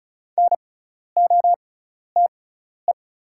Morse code, heard as a keyed tone, about 36 wpm, spaced as for 12 wpm, 700 Hz. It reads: NOTE
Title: 6x Character Spacing - Top 500 Words - 35wpm